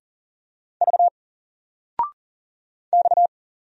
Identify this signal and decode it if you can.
Morse code, heard as a keyed tone, about 39 wpm, 700 Hz. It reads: V X
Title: Single Letter-Number Alt 40wpm